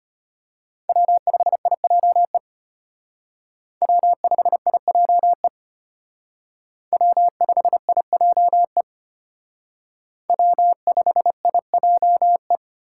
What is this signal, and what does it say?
Morse code, heard as a keyed tone, about 32 wpm, 700 Hz. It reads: W5IJE W5IJE W5IJE W5IJE